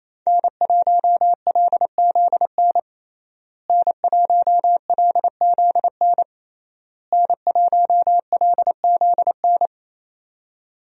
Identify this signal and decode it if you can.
Morse code, heard as a keyed tone, about 27 wpm, 700 Hz. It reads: N1LZN N1LZN N1LZN
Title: USA Call Signs - 5 Characters - Repeated 3x - 28wpm